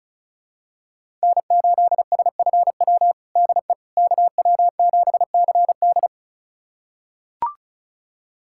Morse code, heard as a keyed tone, about 36 wpm, 700 Hz. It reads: N8SFW DE KW7CD
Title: USA Callsign DE Callsign - 5 Characters - 35wpm